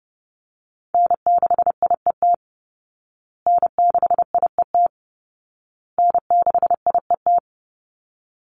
Morse code, heard as a keyed tone, about 30 wpm, 700 Hz. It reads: N6IET N6IET N6IET
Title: Call Signs Repeated 2x - 30wpm